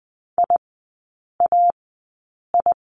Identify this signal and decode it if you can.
Morse code, heard as a keyed tone, about 20 wpm, spaced as for 8 wpm, 700 Hz. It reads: IAI
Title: 035 Lesson 7e - Letter s - Three Characters 20wpm